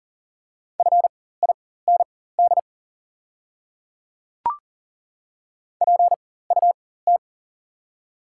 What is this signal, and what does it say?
Morse code, heard as a keyed tone, about 38 wpm, spaced as for 18 wpm, 690 Hz. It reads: FIND PUT
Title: 4x Character Spacing - Top 500 Words - 40wpm